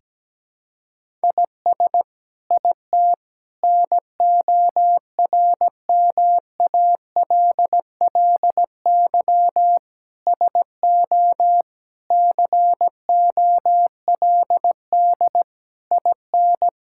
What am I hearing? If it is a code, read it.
Morse code, heard as a keyed tone, about 17 wpm, 700 Hz. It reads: IS IT NORMALLY SO COLD IN
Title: Sentences from Top 1000 Words - Review Part 6 - 17wpm